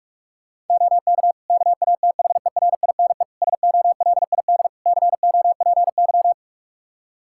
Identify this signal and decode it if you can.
Morse code, heard as a keyed tone, about 46 wpm, 700 Hz. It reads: OK KATHERINE SOLID COPY
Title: K1USN SST QSO Practice 45wpm